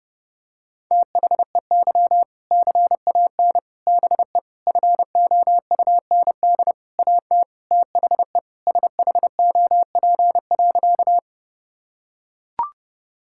Morse code, heard as a keyed tone, about 32 wpm, 700 Hz. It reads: THEY CAN BE FOUND AT THE SHOP.
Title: Sentences from Top 1600 Words 30wpm